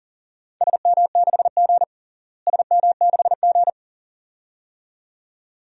Morse code, heard as a keyed tone, about 40 wpm, 700 Hz. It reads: SM6G SM6G
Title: DXCC-100 Call Signs - 4 Characters - Repeated 2x - 40wpm